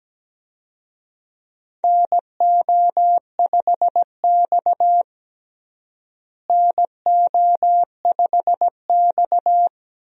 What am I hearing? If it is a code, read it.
Morse code, heard as a keyed tone, about 17 wpm, 700 Hz. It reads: NO5X NO5X
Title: USA Call Signs - 4 Characters - Repeated 2x - 17wpm